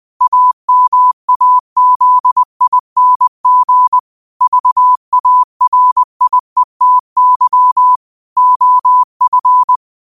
Morse code, heard as a keyed tone, about 20 wpm, 1000 Hz. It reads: AMAZING VARIETY OF